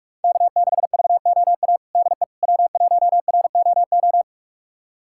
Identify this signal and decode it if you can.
Morse code, heard as a keyed tone, about 45 wpm, 700 Hz. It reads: K6VOA DE W1ROO
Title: USA Callsign DE Callsign - 5 Characters - 45wpm